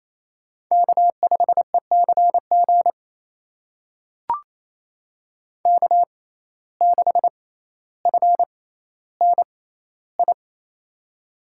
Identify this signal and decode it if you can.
Morse code, heard as a keyed tone, about 28 wpm, spaced as for 14 wpm, 700 Hz. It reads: K5ECG K6FNI
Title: USA Call Signs - 5 Characters - 6x Character Spacing - 28wpm